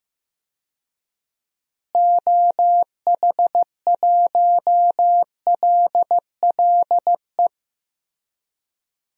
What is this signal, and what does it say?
Morse code, heard as a keyed tone, about 15 wpm, 700 Hz. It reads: OH1LLE
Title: DXCC-100 Call Signs - 6 Characters - Repeated 2x - 15wpm